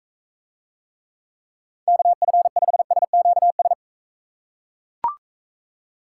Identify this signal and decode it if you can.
Morse code, heard as a keyed tone, about 45 wpm, 700 Hz. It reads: KF5SQS